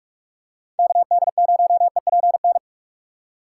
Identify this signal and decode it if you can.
Morse code, heard as a keyed tone, about 44 wpm, 700 Hz. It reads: KD0EPN
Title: USA Call Signs - 6 Characters - 6x Character Spacing - 45wpm